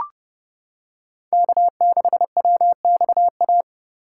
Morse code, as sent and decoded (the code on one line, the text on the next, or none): -.- -.... .-- -..- .-
K6WXA